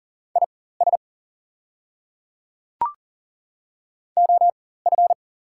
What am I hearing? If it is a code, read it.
Morse code, heard as a keyed tone, about 40 wpm, spaced as for 19 wpm, 690 Hz. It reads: IS OF
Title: 4x Character Spacing - Top 100 Words - 40wpm